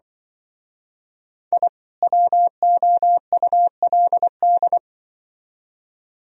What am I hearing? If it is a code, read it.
Morse code, heard as a keyed tone, about 25 wpm, 700 Hz. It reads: I WOULD
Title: Speed-Racing - Sets of 2 Words 20wpm